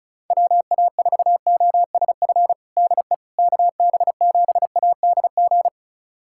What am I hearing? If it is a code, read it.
Morse code, heard as a keyed tone, about 36 wpm, 700 Hz. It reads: WA4OSF DE KB7ADG
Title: USA Callsign DE Callsign - 6 Characters - 35wpm